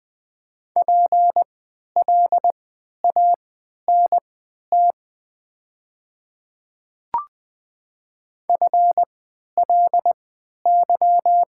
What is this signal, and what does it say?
Morse code, heard as a keyed tone, about 20 wpm, spaced as for 11 wpm, 700 Hz. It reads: PLANT FLY
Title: 3x Character Spacing - Top 500 Words - 20wpm